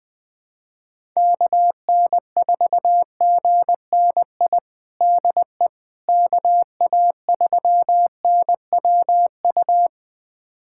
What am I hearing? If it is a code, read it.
Morse code, heard as a keyed tone, about 21 wpm, 700 Hz. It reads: KN4GNI DE KA3NWU
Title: USA Callsign DE Callsign - 6 Characters - 20wpm